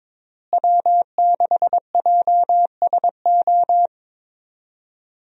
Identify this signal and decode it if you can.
Morse code, heard as a keyed tone, about 22 wpm, 700 Hz. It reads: W6JSO
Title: USA Call Signs - 5 Characters - Repeated 3x - 22wpm